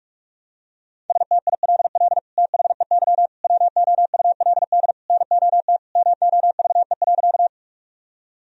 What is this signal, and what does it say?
Morse code, heard as a keyed tone, about 41 wpm, 700 Hz. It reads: STILL THEY WOULD NOT MOVE.
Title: Sentences from Top 1000 Words - Review Part 1 - 45wpm